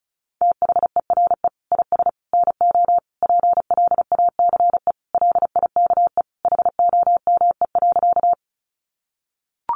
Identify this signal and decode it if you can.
Morse code, heard as a keyed tone, about 34 wpm, 700 Hz. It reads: THERE IS NO PLACE LIKE HOME.
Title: Sentences from Top 200 Words 35wpm